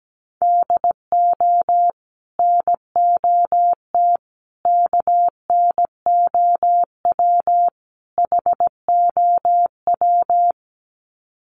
Morse code, heard as a keyed tone, about 17 wpm, 700 Hz. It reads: DO NOT KNOW HOW
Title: Sets of 4 Words 17wpm